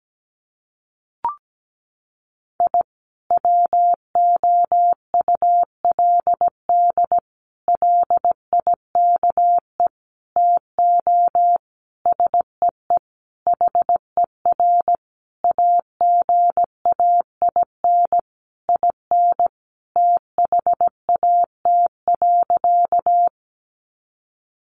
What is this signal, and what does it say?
Morse code, heard as a keyed tone, about 17 wpm, 700 Hz. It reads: I WOULD LIKE TO SEE HER AGAIN IN THAT.
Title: Sentences from Top 200 Words 17wpm